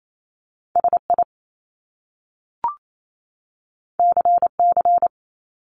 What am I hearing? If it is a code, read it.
Morse code, heard as a keyed tone, about 28 wpm, 700 Hz. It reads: SI CC
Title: Top Two Letter Combinations in Words 28wpm